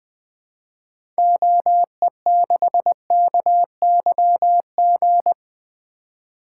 Morse code, heard as a keyed tone, about 20 wpm, 700 Hz. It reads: OE6KYG
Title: 2023 SOTA Activator Call Signs - 20wpm